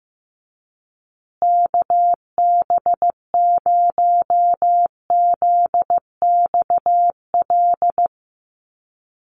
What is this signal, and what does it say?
Morse code, heard as a keyed tone, about 15 wpm, 700 Hz. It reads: KB0ZXL
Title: USA Call Signs Repeated 2x 15wpm